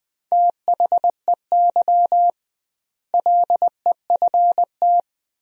Morse code, heard as a keyed tone, about 20 wpm, 700 Hz. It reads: THEY LEFT
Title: 2x Word-Spacing - Sentences from Top 400 Words - 20wpm